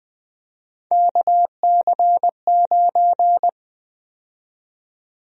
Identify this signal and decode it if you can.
Morse code, heard as a keyed tone, about 20 wpm, 700 Hz. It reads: KC9
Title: USA Call Sign Prefixes - 20wpm